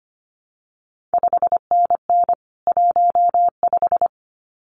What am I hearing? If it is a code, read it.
Morse code, heard as a keyed tone, about 25 wpm, 700 Hz. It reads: 5NN 15
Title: CQ WW DX Contest Exchange Encore - 25wpm